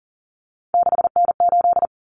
Morse code, heard as a keyed tone, about 40 wpm, 700 Hz. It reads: 6N8
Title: Common Cut Numbers - 3-Digit 40wpm